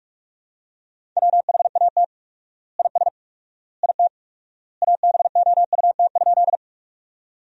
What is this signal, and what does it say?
Morse code, heard as a keyed tone, about 46 wpm, 700 Hz. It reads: WHAT IS IT ABOUT?
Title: 4x Word-Spacing - Sentences from Top 100 Words - 45wpm